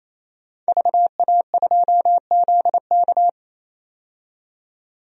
Morse code, heard as a keyed tone, about 28 wpm, 700 Hz. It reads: VA2ZK
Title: DXCC-100 Call Signs - 5 Characters - 28wpm